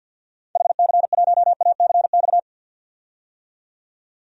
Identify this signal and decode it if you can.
Morse code, heard as a keyed tone, about 48 wpm, 700 Hz. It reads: HC1ACX
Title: DXCC-100 Call Signs - 6 Characters - 50wpm